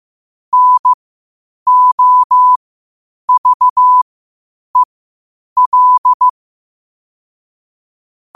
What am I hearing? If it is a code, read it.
Morse code, heard as a keyed tone, about 15 wpm, spaced as for 8 wpm, 1000 Hz. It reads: NOVEL
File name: 8wpm